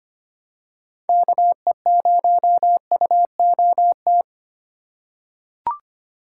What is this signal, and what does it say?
Morse code, heard as a keyed tone, about 25 wpm, 700 Hz. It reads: KE0UOT
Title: USA Call Signs - 6 Characters - 3x Character Spacing - 25wpm